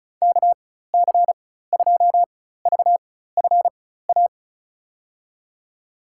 Morse code, heard as a keyed tone, about 34 wpm, spaced as for 16 wpm, 690 Hz. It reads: KC2VFA
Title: USA Call Signs - 6 Characters - 4x Character Spacing - 35wpm